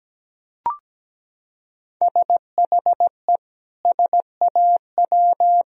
Morse code, heard as a keyed tone, about 17 wpm, 700 Hz. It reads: SHE SAW
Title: Sentences from Top 1000 Words - Review Part 4 - 17wpm